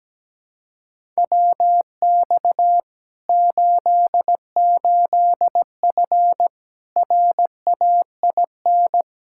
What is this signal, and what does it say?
Morse code, heard as a keyed tone, about 17 wpm, 700 Hz. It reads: WX 88F RAIN
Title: Practice QSO - Exchange 2 - Info - 17wpm